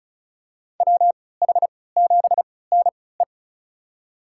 Morse code, heard as a keyed tone, about 36 wpm, spaced as for 20 wpm, 700 Hz. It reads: WH7NE
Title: USA Call Signs - 5 Characters - 3x Character Spacing - 35wpm